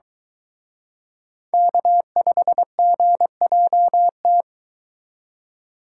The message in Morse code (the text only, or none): K5GJT